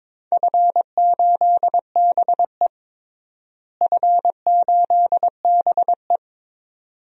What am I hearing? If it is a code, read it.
Morse code, heard as a keyed tone, about 22 wpm, 700 Hz. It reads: F8BE F8BE
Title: DXCC-100 Call Signs - 4 Characters - Repeated 2x - 22wpm